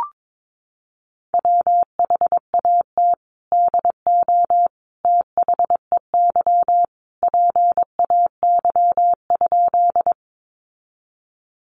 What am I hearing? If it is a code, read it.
Morse code, heard as a keyed tone, about 22 wpm, 700 Hz. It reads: WHAT DO THEY PAY?
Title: Sentences from Top 700 Words 22wpm